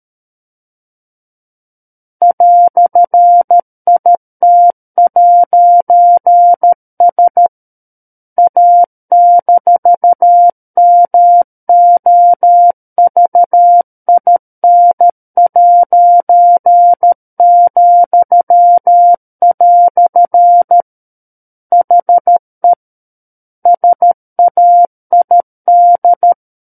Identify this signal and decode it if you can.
Morse code, heard as a keyed tone, about 13 wpm, 700 Hz. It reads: "IT'S A-MOVIN'," HE SAID
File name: Book One / 04-Chapter 4 - 13 WPM